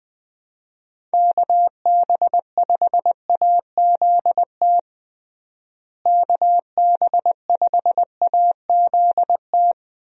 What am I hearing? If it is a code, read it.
Morse code, heard as a keyed tone, about 20 wpm, 700 Hz. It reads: KB5AZT KB5AZT
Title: USA Call Signs - 6 Characters - Repeated 3x - 20wpm